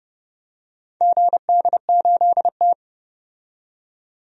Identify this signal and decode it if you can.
Morse code, heard as a keyed tone, about 31 wpm, 700 Hz. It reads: GD8T
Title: DXCC-100 Call Signs - 4 Characters - 30wpm